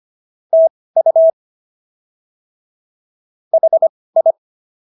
Morse code, heard as a keyed tone, about 25 wpm, spaced as for 18 wpm, 650 Hz. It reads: TU HI